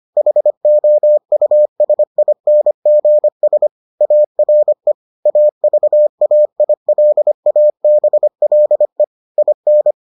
Morse code, heard as a keyed tone, about 25 wpm, 600 Hz. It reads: HOUSINGS ARE AVAILABLE IN